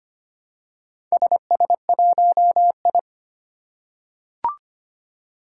Morse code, heard as a keyed tone, about 25 wpm, 700 Hz. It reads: SS1I
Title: Four Challenging Letter-Number Combinations 25wpm